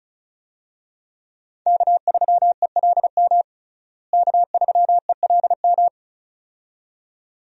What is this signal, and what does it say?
Morse code, heard as a keyed tone, about 34 wpm, 700 Hz. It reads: K3ELM K3ELM
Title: USA Call Signs - 5 Characters - Repeated 2x - 35wpm